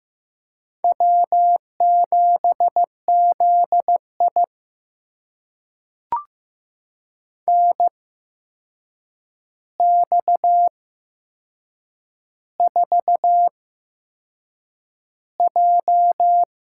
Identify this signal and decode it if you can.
Morse code, heard as a keyed tone, about 15 wpm, spaced as for 6 wpm, 700 Hz. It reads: W7ZI NX4J
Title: USA Call Signs - 4 Characters - 8x Character Spacing - 15wpm